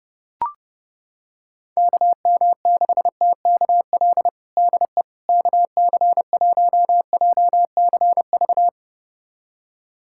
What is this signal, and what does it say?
Morse code, heard as a keyed tone, about 30 wpm, 700 Hz. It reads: KM6TKL DE KC1JCV
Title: USA Callsign DE Callsign 30wpm